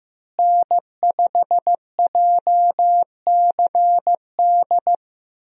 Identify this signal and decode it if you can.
Morse code, heard as a keyed tone, about 16 wpm, 700 Hz. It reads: N5JCD
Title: USA Call Signs - 5 Characters - Repeated 2x - 15wpm